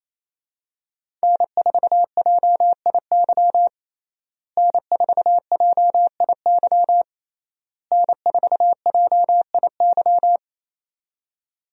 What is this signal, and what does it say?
Morse code, heard as a keyed tone, about 28 wpm, 700 Hz. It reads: N4JIY N4JIY N4JIY
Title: USA Call Signs - 5 Characters - Repeated 3x - 28wpm